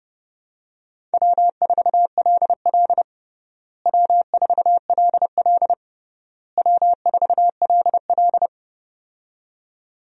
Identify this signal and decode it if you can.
Morse code, heard as a keyed tone, about 30 wpm, 700 Hz. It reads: W4LL W4LL W4LL
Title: USA Call Signs - 4 Characters - Repeated 3x - 30wpm